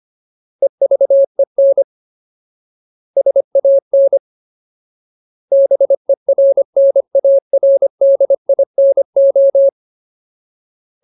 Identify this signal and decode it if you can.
Morse code, heard as a keyed tone, about 25 wpm, 550 Hz. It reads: EVEN SAN BERNARDINO
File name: Ultimate Word List / 25WPM Ultimate